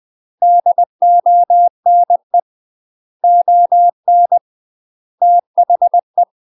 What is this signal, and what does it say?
Morse code, heard as a keyed tone, about 20 wpm, 700 Hz. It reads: DONE ON THE
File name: UK MASTER QSOs TRAINING/20WPM